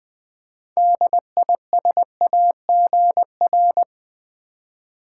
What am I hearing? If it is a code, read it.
Morse code, heard as a keyed tone, about 20 wpm, 700 Hz. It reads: DISAGR
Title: Build A Word - 6 Letters At A Time - 12-Letter Words - 20wpm